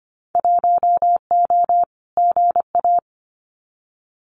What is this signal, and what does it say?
Morse code, heard as a keyed tone, about 25 wpm, 690 Hz. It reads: 1O GA
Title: Winter Field Day Exchanges 25wpm